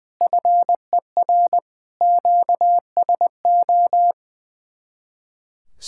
Morse code, heard as a keyed tone, about 19 wpm, 700 Hz. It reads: FER QSO
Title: Speed-Racing Common QSO Phrases - 20wpm